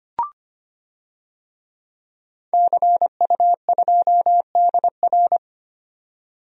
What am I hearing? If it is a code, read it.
Morse code, heard as a keyed tone, about 25 wpm, 700 Hz. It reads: CU2DR
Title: DXCC-100 Call Signs - 5 Characters - 25wpm